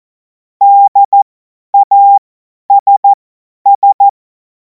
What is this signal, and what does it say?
Morse code, heard as a keyed tone, about 14 wpm, spaced as for 10 wpm, 800 Hz. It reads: DASS
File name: Morsekurs / Witze / Witz 10. Februar 2026 50 F70